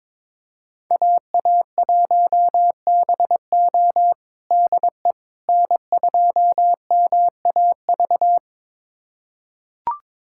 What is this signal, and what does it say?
Morse code, heard as a keyed tone, about 22 wpm, 700 Hz. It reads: AA1BO DE N2MAV
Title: USA Callsign DE Callsign - 5 Characters - 22wpm